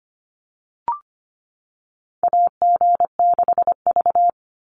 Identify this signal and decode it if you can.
Morse code, heard as a keyed tone, about 25 wpm, 700 Hz. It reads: AG6V
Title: NA SOTA Chaser Callsigns 25wpm